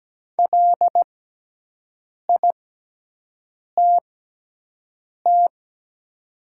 Morse code, heard as a keyed tone, about 17 wpm, spaced as for 6 wpm, 700 Hz. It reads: LITT
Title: 6x Character Spacing - Top 500 Words - 17wpm